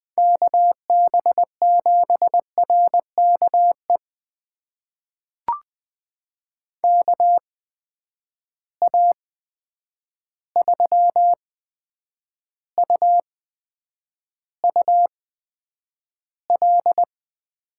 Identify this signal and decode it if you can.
Morse code, heard as a keyed tone, about 20 wpm, spaced as for 9 wpm, 700 Hz. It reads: KB7RKE KA3UUL
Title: USA Call Signs - 6 Characters - 8x Character Spacing - 20wpm